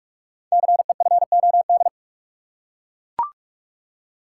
Morse code, heard as a keyed tone, about 44 wpm, 700 Hz. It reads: CEFOD
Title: Code Groups - 5 Letters - 45wpm